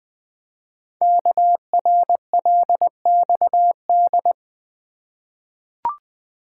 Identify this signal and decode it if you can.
Morse code, heard as a keyed tone, about 20 wpm, 700 Hz. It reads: KRLXD
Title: Code Groups - 5 Letters - 20wpm